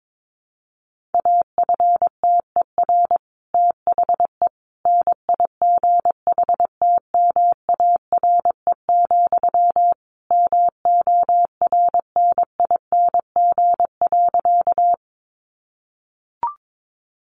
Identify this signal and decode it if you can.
Morse code, heard as a keyed tone, about 22 wpm, 700 Hz. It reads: AFTER THE NIGHTMARE, MORNING.
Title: A Sound Of Thunder 22wpm